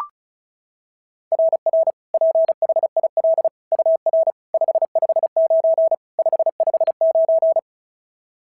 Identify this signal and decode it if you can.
Morse code, heard as a keyed tone, about 36 wpm, 690 Hz. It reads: RR PHIL UR 559 559
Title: SOTA QSO Practice 35wpm